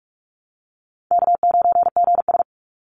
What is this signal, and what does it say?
Morse code, heard as a keyed tone, about 46 wpm, 700 Hz. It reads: K9GS
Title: Call Signs - 45wpm